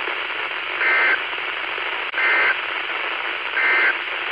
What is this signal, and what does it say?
Emergency Alert System (SAME) tones, heard as harsh buzzing bursts. It is heard from 0.7 to 4.0 s.
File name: Washtenaw County SKYWARN N8DUY and W8SRC Repeaters